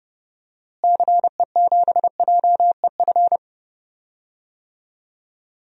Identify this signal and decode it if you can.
Morse code, heard as a keyed tone, about 32 wpm, 700 Hz. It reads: CE7JEF
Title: DXCC-100 Call Signs - 6 Characters - Repeated 2x - 30wpm